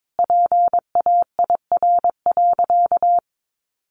Morse code, heard as a keyed tone, about 22 wpm, 700 Hz. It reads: PAIR.
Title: Sentences from Top 1200 Words 22wpm